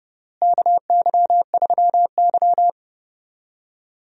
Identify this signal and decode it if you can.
Morse code, heard as a keyed tone, about 30 wpm, 700 Hz. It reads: KY3Y
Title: USA Call Signs - 4 Characters - Repeated 2x - 30wpm